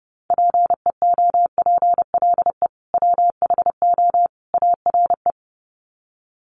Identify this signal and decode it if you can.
Morse code, heard as a keyed tone, about 30 wpm, 700 Hz. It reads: PEOPLE WHO ARE
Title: Sets of 3 Words 30wpm